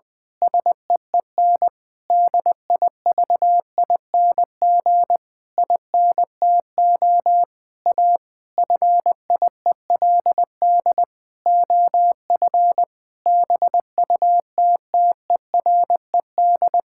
Morse code, heard as a keyed tone, about 21 wpm, 700 Hz. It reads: SEEN DIVING INTO A FIELD OF BUTTERED
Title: Pangram Challenge v2 - 20wpm